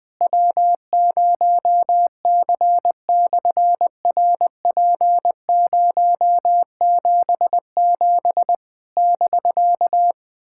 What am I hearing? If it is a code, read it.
Morse code, heard as a keyed tone, about 20 wpm, 700 Hz. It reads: W0C/RP077 <BK>
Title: SOTA QSO Practice 20wpm